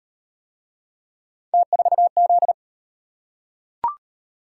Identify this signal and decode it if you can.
Morse code, heard as a keyed tone, about 36 wpm, 700 Hz. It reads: T4Z